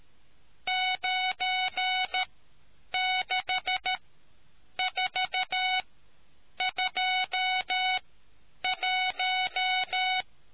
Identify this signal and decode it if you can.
Morse code, heard as a keyed tone, about 13 wpm, spaced as for 8 wpm, 740 Hz. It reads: 96421